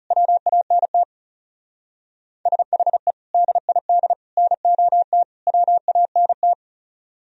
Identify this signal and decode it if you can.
Morse code, heard as a keyed tone, about 38 wpm, 700 Hz. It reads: WANT SHE DID NOT WANT